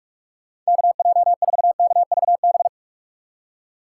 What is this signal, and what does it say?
Morse code, heard as a keyed tone, about 46 wpm, 700 Hz. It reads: KJ4KVB
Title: USA Call Signs - 6 Characters - 6x Character Spacing - 45wpm